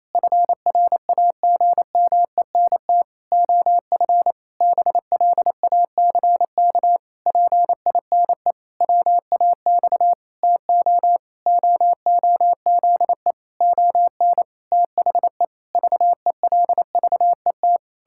Morse code, heard as a keyed tone, about 27 wpm, 700 Hz. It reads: FRAGMENT OF BLACK PINE WAX TO OOZE ON THE VELVET
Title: Pangram Challenge v2 - 28wpm